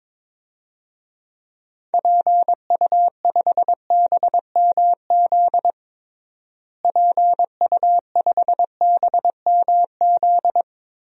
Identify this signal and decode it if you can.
Morse code, heard as a keyed tone, about 22 wpm, 700 Hz. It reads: PU5BMZ PU5BMZ
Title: DXCC-100 Call Signs - 6 Characters - Repeated 3x - 22wpm